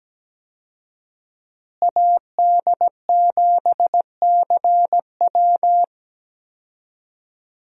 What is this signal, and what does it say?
Morse code, heard as a keyed tone, about 17 wpm, 700 Hz. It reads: AD7CW
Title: USA Call Signs - 5 Characters - Repeated 3x - 17wpm